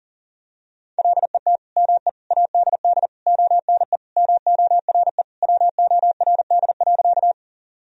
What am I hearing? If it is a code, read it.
Morse code, heard as a keyed tone, about 40 wpm, 700 Hz. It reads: LET ME ADD ONE MORE WORD.
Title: Sentences from Top 1000 Words - Review Part 1 - 40wpm